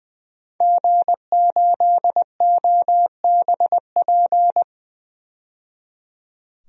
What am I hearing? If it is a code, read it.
Morse code, heard as a keyed tone, about 20 wpm, 700 Hz. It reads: G8OBP
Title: DXCC-100 Call Signs - 5 Characters - 20wpm